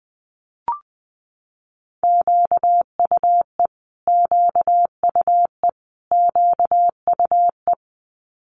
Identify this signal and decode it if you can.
Morse code, heard as a keyed tone, about 20 wpm, 700 Hz. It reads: QUE QUE QUE
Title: Multi-letter Phonemes Repeated 3x 20wpm